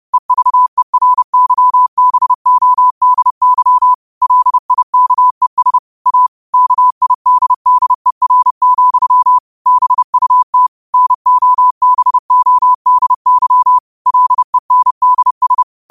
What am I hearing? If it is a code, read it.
Morse code, heard as a keyed tone, about 30 wpm, 1000 Hz. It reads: EVERYBODY LIKES A KIDDER, BUT NOBODY LENDS
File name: 30wpm